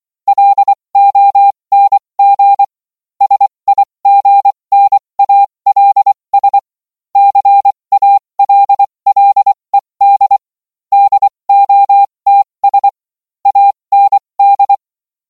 Morse code, heard as a keyed tone, about 24 wpm, spaced as for 20 wpm, 800 Hz. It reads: LONG SIGNALS CALLED DOTS AND